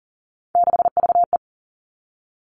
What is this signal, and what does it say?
Morse code, heard as a keyed tone, about 38 wpm, 700 Hz. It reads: 6VE
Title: Three Challenging Letter-Number Combinations 40wpm